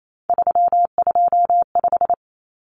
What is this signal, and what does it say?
Morse code, heard as a keyed tone, about 28 wpm, 700 Hz. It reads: 325
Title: Numbers - 3-Digit 28wpm